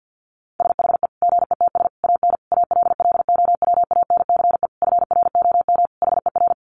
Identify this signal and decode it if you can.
Morse code, heard as a keyed tone, about 48 wpm, 700 Hz. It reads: SHE GETS AN ALLOWANCE FROM HER
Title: Easy Phrases 50wpm